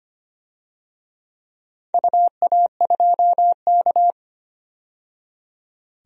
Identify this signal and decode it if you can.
Morse code, heard as a keyed tone, about 25 wpm, 700 Hz. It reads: UA2K
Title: DXCC-100 Call Signs - 4 Characters - 25wpm